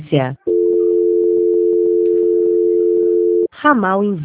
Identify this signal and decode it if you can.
Dial tone (North America), 350 + 440 Hz, with speech.
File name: out- -2029- - - .15